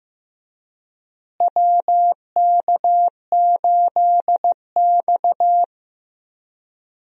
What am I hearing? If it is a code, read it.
Morse code, heard as a keyed tone, about 16 wpm, 700 Hz. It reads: WK8X
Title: USA Call Signs - 4 Characters - Repeated 3x - 15wpm